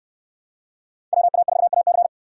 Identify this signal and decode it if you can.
Morse code, heard as a keyed tone, about 68 wpm, 700 Hz. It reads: WA4AC